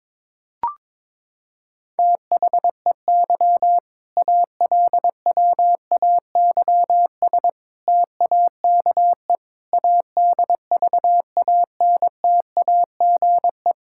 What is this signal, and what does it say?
Morse code, heard as a keyed tone, about 22 wpm, 700 Hz. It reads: THEY ALWAYS TAKE ADVANTAGE
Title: Sentences from Top 1000 Words - Review Part 6 - 22wpm